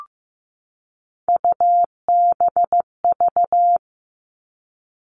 Morse code, heard as a keyed tone, about 15 wpm, 700 Hz. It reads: UBV
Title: All Cut Numbers - 3-Digit 15wpm